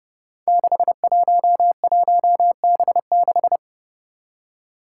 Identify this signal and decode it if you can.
Morse code, heard as a keyed tone, about 30 wpm, 700 Hz. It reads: 611B6
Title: Five Challenging Letter-Number Combinations 30wpm